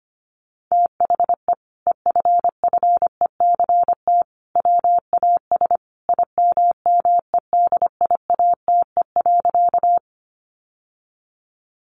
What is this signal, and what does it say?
Morse code, heard as a keyed tone, about 25 wpm, 700 Hz. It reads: THE EFFECT WAS IMMEDIATE.
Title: Sentences from Top 1200 Words 25wpm